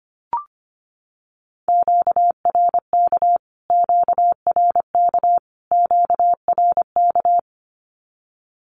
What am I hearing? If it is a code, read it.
Morse code, heard as a keyed tone, about 25 wpm, 700 Hz. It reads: QRK QRK QRK
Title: Q-Codes Repeated 3x 25wpm